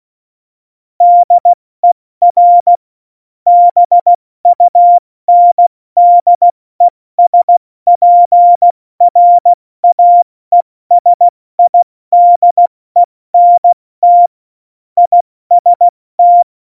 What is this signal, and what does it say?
Morse code, heard as a keyed tone, about 16 wpm, 700 Hz. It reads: DER BUNDESPRAESIDENT IST
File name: Morsekurs / Witze / Witz 19 Mai, 2020 70 F80